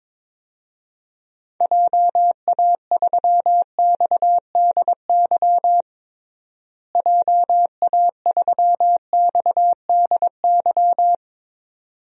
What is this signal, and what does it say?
Morse code, heard as a keyed tone, about 21 wpm, 700 Hz. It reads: JA3XDY JA3XDY
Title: DXCC-100 Call Signs - 6 Characters - Repeated 2x - 22wpm